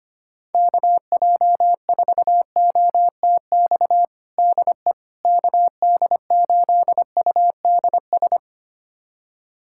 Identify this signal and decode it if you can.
Morse code, heard as a keyed tone, about 25 wpm, 700 Hz. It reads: KJ4OTX DE KD8UDS
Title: USA Callsign DE Callsign - 6 Characters - 25wpm